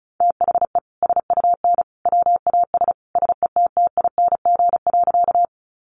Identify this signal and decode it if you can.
Morse code, heard as a keyed tone, about 34 wpm, 700 Hz. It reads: THE SUN WAS SETTING.
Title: Sentences from Top 1300 Words 35wpm